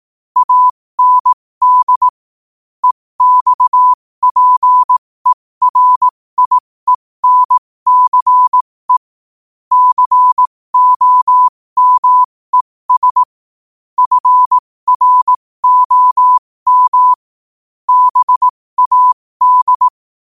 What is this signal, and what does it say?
Morse code, heard as a keyed tone, about 18 wpm, spaced as for 15 wpm, 1000 Hz. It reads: AND EXPERIENCE COMES FROM BAD
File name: qotdfiles / 15wpm